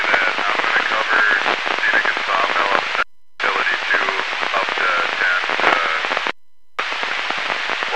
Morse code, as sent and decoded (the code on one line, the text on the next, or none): none